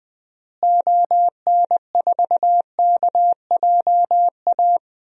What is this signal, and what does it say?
Morse code, heard as a keyed tone, about 20 wpm, 700 Hz. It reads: ON4KJA
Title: DXCC-100 Call Signs - 6 Characters - Repeated 2x - 20wpm